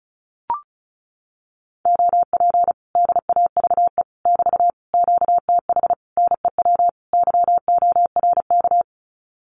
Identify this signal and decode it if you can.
Morse code, heard as a keyed tone, about 34 wpm, 700 Hz. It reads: OP DAVE <BT> QTH NEW YORK
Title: Common QSO Phrases Encore 35wpm